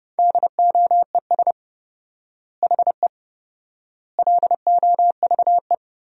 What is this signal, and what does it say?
Morse code, heard as a keyed tone, about 30 wpm, 700 Hz. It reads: DOES HE LOVE
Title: 4x Word-Spacing - Sentences from Top 400 Words - 30wpm